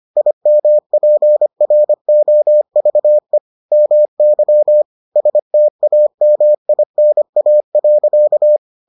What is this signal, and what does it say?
Morse code, heard as a keyed tone, about 25 wpm, 600 Hz. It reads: IMPROVE MY STAMINA.